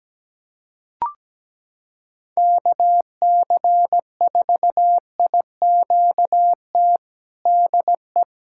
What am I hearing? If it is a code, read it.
Morse code, heard as a keyed tone, about 17 wpm, 700 Hz. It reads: KC4IQT DE
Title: USA Callsign DE Callsign 17wpm